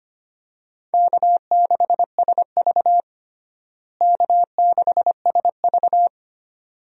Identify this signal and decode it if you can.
Morse code, heard as a keyed tone, about 25 wpm, 700 Hz. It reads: K6SV K6SV
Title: USA Call Signs - 4 Characters - Repeated 2x - 25wpm